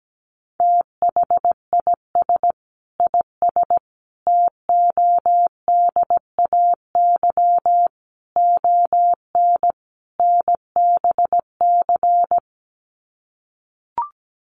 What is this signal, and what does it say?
Morse code, heard as a keyed tone, about 17 wpm, 700 Hz. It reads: THIS IS TODAY ON NBC
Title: Sets of 5 Words Encore 17wpm